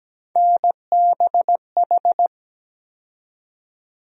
Morse code, heard as a keyed tone, about 17 wpm, 700 Hz. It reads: NBH
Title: USA Call Sign Postfixes - 3 Letters 17wpm